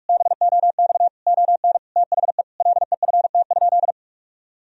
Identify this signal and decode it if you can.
Morse code, heard as a keyed tone, about 46 wpm, 700 Hz. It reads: BOX ON THE LEFT?
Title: Sentences from Top 1000 Words - Review Part 2 - 45wpm